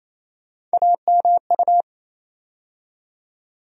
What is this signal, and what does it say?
Morse code, heard as a keyed tone, about 28 wpm, 700 Hz. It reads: AMU
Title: USA Call Sign Postfixes - 3 Letters 28wpm